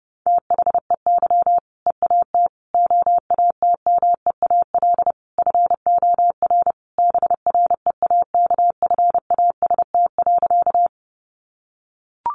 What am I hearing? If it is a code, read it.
Morse code, heard as a keyed tone, about 30 wpm, 700 Hz. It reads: THEY EAT OATMEAL FOR BREAKFAST.
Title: Easy Phrases 30wpm